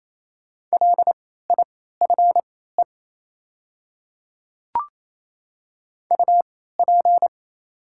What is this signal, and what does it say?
Morse code, heard as a keyed tone, about 27 wpm, spaced as for 16 wpm, 700 Hz. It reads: LIFE UP
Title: 3x Character Spacing - Top 300 Words - 28wpm